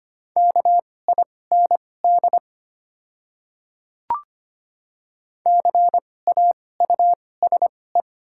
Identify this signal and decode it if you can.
Morse code, heard as a keyed tone, about 25 wpm, spaced as for 18 wpm, 700 Hz. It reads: KIND CAUSE
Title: 2x Character Spacing - Top 500 Words - 25wpm